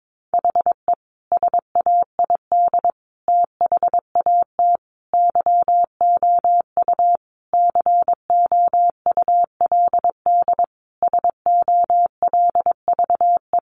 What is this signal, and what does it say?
Morse code, heard as a keyed tone, about 22 wpm, 700 Hz. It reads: HE SAID THAT YOU COULD SOLVE
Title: Sentences from Top 1000 Words 22wpm